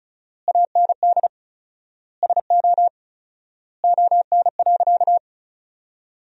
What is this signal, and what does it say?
Morse code, heard as a keyed tone, about 36 wpm, 700 Hz. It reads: AND SO ON.
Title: 4x Word-Spacing - Sentences from Top 100 Words - 35wpm